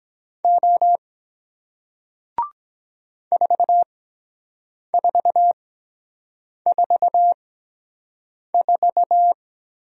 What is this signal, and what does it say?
Morse code, heard as a keyed tone, about 23 wpm, spaced as for 7 wpm, 700 Hz. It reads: O 4444